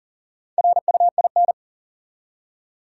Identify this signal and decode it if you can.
Morse code, heard as a keyed tone, about 40 wpm, 700 Hz. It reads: RUIN
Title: Four Letter Words - 40wpm